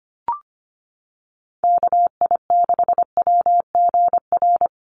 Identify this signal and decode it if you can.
Morse code, heard as a keyed tone, about 25 wpm, 700 Hz. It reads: KI6WGR
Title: USA Call Signs 25wpm